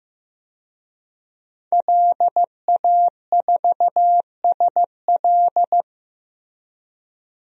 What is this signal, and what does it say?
Morse code, heard as a keyed tone, about 15 wpm, 700 Hz. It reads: LA4SL
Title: DXCC-100 Call Signs - 5 Characters - Repeated 3x - 15wpm